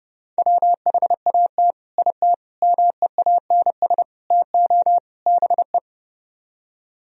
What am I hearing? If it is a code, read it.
Morse code, heard as a keyed tone, about 38 wpm, 700 Hz. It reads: WHAT IT MEANS TO BE